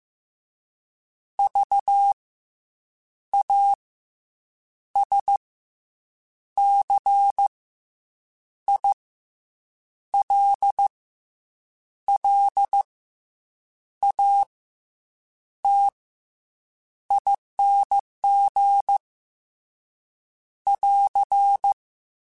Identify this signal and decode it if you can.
Morse code, heard as a keyed tone, about 15 wpm, spaced as for 7 wpm, 800 Hz. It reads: VASCILLATING <AR>